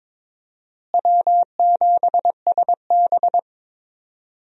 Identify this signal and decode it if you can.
Morse code, heard as a keyed tone, about 22 wpm, 700 Hz. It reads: W7SB
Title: USA Call Signs - 4 Characters - Repeated 2x - 22wpm